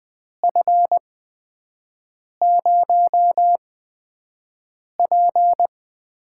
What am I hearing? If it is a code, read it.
Morse code, heard as a keyed tone, about 20 wpm, spaced as for 5 wpm, 700 Hz. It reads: F0P
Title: USA Call Signs - 5 Characters - 8x Character Spacing - 20wpm